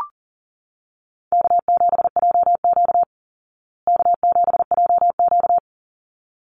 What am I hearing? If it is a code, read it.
Morse code, heard as a keyed tone, about 40 wpm, 700 Hz. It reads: K7JQ K7JQ
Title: Easy Call Signs Repeated 2x - 40wpm